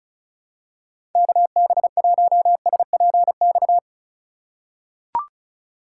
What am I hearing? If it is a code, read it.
Morse code, heard as a keyed tone, about 34 wpm, 700 Hz. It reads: KB1SPX
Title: USA Call Signs - 2x Character Spacing - 35wpm